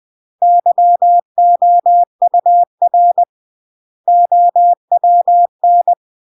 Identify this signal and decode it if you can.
Morse code, heard as a keyed tone, about 20 wpm, 700 Hz. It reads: YOUR OWN
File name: CW / jodorowsky / 82 commandments 20-20